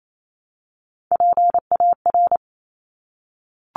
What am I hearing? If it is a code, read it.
Morse code, heard as a keyed tone, about 28 wpm, 700 Hz. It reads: PAR
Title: Top Three Letter Combinations in Words 28wpm